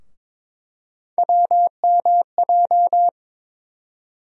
Morse code, heard as a keyed tone, about 22 wpm, 700 Hz. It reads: WMJ
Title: USA Call Sign Postfixes - 3 Letters 22wpm